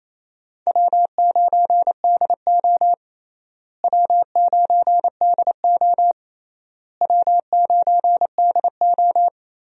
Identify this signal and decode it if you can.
Morse code, heard as a keyed tone, about 28 wpm, 700 Hz. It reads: W9DO W9DO W9DO
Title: USA Call Signs - 4 Characters - Repeated 3x - 28wpm